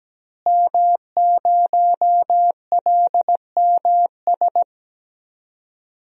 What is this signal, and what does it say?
Morse code, heard as a keyed tone, about 17 wpm, 700 Hz. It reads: M0LMS
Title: DXCC-100 Call Signs - 5 Characters - Repeated 2x - 17wpm